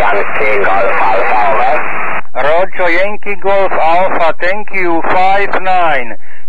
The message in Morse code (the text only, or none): none